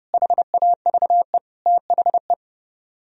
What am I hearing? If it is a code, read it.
Morse code, heard as a keyed tone, about 37 wpm, 700 Hz. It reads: HAVE THE
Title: Speed-Racing - Sets of 4 Words 30wpm